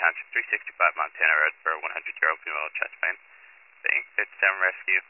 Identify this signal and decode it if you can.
Emergency Alert System (SAME) tones, heard as harsh buzzing bursts, with speech. It is heard from 1.2 to 4.8 s.